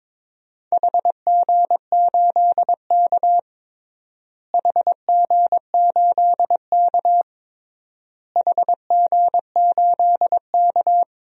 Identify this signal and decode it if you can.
Morse code, heard as a keyed tone, about 22 wpm, 700 Hz. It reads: HG8K HG8K HG8K
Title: DXCC-100 Call Signs - 4 Characters - Repeated 3x - 22wpm